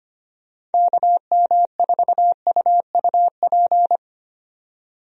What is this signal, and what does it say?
Morse code, heard as a keyed tone, about 25 wpm, 700 Hz. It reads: KM4UUP
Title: USA Call Signs - 6 Characters - 25wpm